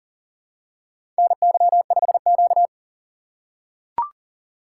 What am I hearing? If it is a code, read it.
Morse code, heard as a keyed tone, about 39 wpm, 700 Hz. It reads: NY5Q
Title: USA Call Signs - 4 Characters - 8x Character Spacing - 40wpm